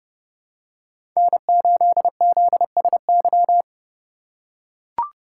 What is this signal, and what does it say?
Morse code, heard as a keyed tone, about 31 wpm, 700 Hz. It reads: N8ZSY